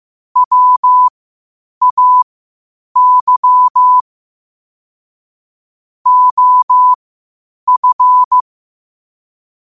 Morse code, heard as a keyed tone, about 15 wpm, spaced as for 8 wpm, 1000 Hz. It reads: WAY OF
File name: qotdfiles / 8wpm